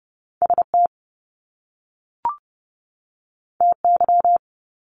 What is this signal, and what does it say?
Morse code, heard as a keyed tone, about 30 wpm, 700 Hz. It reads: ST TY
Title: Top Two Letter Combinations in Words 30wpm